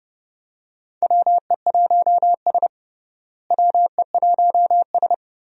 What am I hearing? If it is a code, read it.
Morse code, heard as a keyed tone, about 30 wpm, 700 Hz. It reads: WE1S WE1S
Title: USA Call Signs - 4 Characters - Repeated 2x - 30wpm